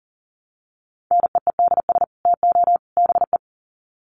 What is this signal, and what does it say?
Morse code, heard as a keyed tone, about 40 wpm, 700 Hz. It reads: NEEDS TO BE
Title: Sets of 3 Words 40wpm